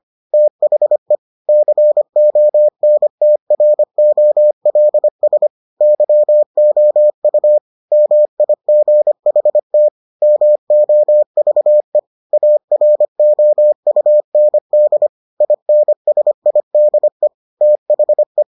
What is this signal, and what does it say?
Morse code, heard as a keyed tone, about 25 wpm, 600 Hz. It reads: THE CONTROLS YOU MIGHT MOVE AROUND INSIDE THE